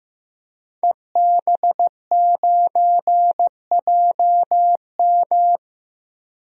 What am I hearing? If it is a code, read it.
Morse code, heard as a keyed tone, about 16 wpm, 700 Hz. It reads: EB9JM
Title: DXCC-100 Call Signs - 5 Characters - 15wpm